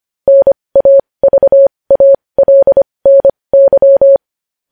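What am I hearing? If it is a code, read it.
Morse code, heard as a keyed tone, about 25 wpm, spaced as for 20 wpm, 550 Hz. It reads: NAVALNY
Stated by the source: Morse code news headlines.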